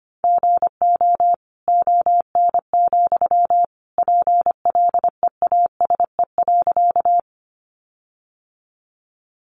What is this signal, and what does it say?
Morse code, heard as a keyed tone, about 25 wpm, 700 Hz. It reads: GO ON, PLEASE.
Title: Sentences from Top 1500 Words 25wpm